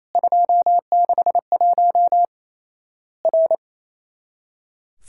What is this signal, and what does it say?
Morse code, heard as a keyed tone, about 28 wpm, 700 Hz. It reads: 261 R
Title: Worked All Europe DX Contest - QTC Exchange - 28wpm